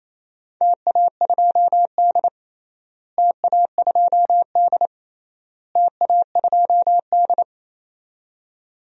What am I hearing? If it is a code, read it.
Morse code, heard as a keyed tone, about 27 wpm, 700 Hz. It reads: TA2D TA2D TA2D
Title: DXCC-100 Call Signs - 4 Characters - Repeated 3x - 28wpm